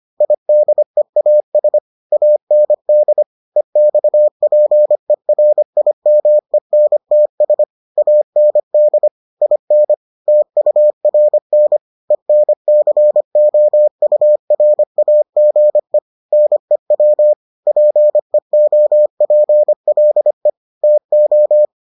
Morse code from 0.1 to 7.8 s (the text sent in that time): IDEAS AND EXPERIMENTS